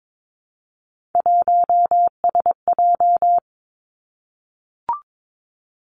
Morse code, heard as a keyed tone, about 22 wpm, 700 Hz. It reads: 1SJ
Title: Three Challenging Letter-Number Combinations 22wpm